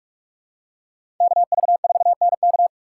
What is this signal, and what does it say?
Morse code, heard as a keyed tone, about 46 wpm, 700 Hz. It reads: KV4NK
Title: USA Call Signs - 5 Characters - 4x Character Spacing - 45wpm